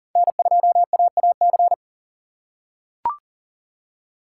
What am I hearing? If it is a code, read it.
Morse code, heard as a keyed tone, about 40 wpm, 700 Hz. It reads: N2AAC
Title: USA Call Signs - 5 Characters - 4x Character Spacing - 40wpm